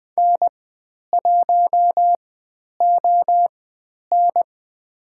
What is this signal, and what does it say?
Morse code, heard as a keyed tone, about 20 wpm, spaced as for 10 wpm, 700 Hz. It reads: N1ON
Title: 095 Lesson 16g - Letter c - Call Signs 20wpm